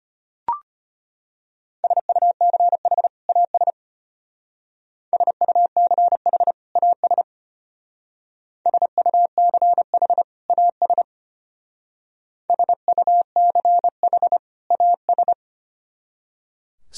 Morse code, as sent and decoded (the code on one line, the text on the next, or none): ... ..- -.-. .... / .- ... / ... ..- -.-. .... / .- ... / ... ..- -.-. .... / .- ... / ... ..- -.-. .... / .- ...
SUCH AS SUCH AS SUCH AS SUCH AS